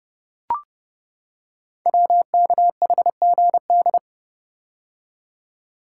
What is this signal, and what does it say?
Morse code, heard as a keyed tone, about 30 wpm, 700 Hz. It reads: WKHGD
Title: Code Groups - 5 Letters - 30wpm